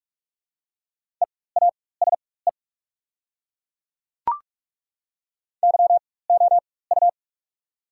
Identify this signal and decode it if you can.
Morse code, heard as a keyed tone, about 42 wpm, spaced as for 21 wpm, 690 Hz. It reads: EASE YOU
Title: 4x Character Spacing - Top 300 Words - 45wpm